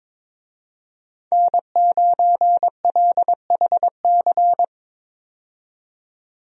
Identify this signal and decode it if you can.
Morse code, heard as a keyed tone, about 22 wpm, 700 Hz. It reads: N9LHC
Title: USA Call Signs - 5 Characters - 22wpm